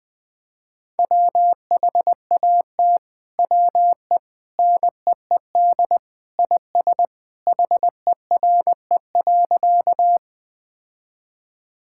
Morse code, heard as a keyed tone, about 20 wpm, 700 Hz. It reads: WHAT WE NEED IS HERE.
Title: Sentences with 2-Word Phrases - 20wpm